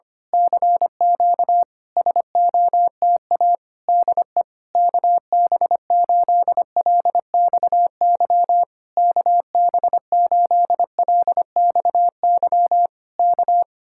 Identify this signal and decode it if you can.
Morse code, heard as a keyed tone, about 24 wpm, 700 Hz. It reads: CQ SOTA DE KB8LXY KB8LXY K
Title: Speed-Racing Common QSO Phrases - 25wpm